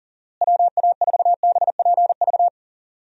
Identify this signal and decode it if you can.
Morse code, heard as a keyed tone, about 40 wpm, 700 Hz. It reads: WA4BPV
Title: USA Call Signs - 6 Characters - Repeated 3x - 40wpm